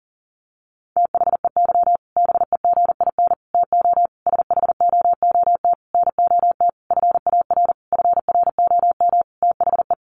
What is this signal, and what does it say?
Morse code, heard as a keyed tone, about 40 wpm, 700 Hz. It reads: THEY BEGIN TO SHOOT NOT FAR FROM THE
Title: Sentences from Top 1200 Words 40wpm